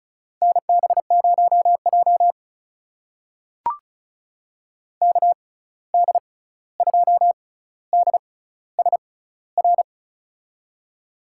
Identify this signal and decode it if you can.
Morse code, heard as a keyed tone, about 36 wpm, spaced as for 16 wpm, 700 Hz. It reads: NB0J KD2DSR
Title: USA Call Signs - 6x Character Spacing - 35wpm